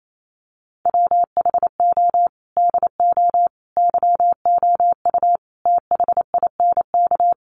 Morse code, heard as a keyed tone, about 28 wpm, 700 Hz. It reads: WHO DO YOU THINK
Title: Sentences from Top 1100 Words 28wpm